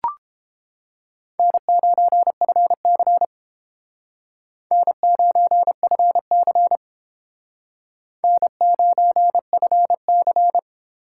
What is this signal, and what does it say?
Morse code, heard as a keyed tone, about 29 wpm, 700 Hz. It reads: N9FC N9FC N9FC